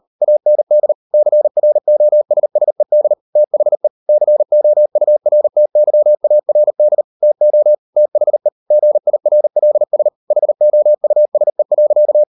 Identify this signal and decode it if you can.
Morse code, heard as a keyed tone, about 38 wpm, 600 Hz. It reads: AND CROSSED THE COURTYARD TO THE GIRLS HOUSE.